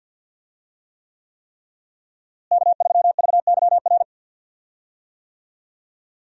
Morse code, heard as a keyed tone, about 52 wpm, 700 Hz. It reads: K3VYR
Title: USA Call Signs - 5 Characters - Repeated 3x - 50wpm